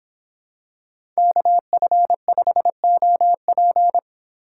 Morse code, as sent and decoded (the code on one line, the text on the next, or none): -.- ..-. ..... --- .--.
KF5OP